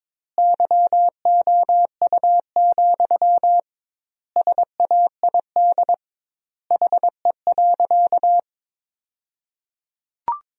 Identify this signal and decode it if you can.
Morse code, heard as a keyed tone, about 21 wpm, 700 Hz. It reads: YOU, SAID HE.
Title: 2x Word-Spacing - Sentences from Top 400 Words - 22wpm